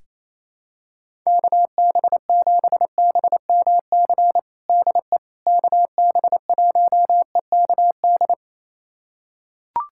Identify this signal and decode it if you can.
Morse code, heard as a keyed tone, about 27 wpm, 700 Hz. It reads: KB7BMC DE KB1EKD
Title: USA Callsign DE Callsign - 6 Characters - 28wpm